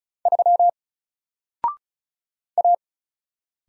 Morse code, heard as a keyed tone, about 35 wpm, 700 Hz. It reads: 3 A
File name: Single-Letter-Number-35wpm